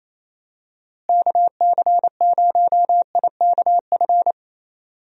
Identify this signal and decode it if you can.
Morse code, heard as a keyed tone, about 28 wpm, 700 Hz. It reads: KC0IKF
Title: USA Call Signs - 6 Characters - Repeated 3x - 28wpm